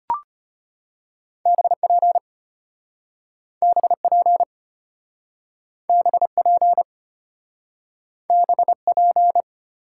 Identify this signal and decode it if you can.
Morse code, heard as a keyed tone, about 34 wpm, 700 Hz. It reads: BP BP BP BP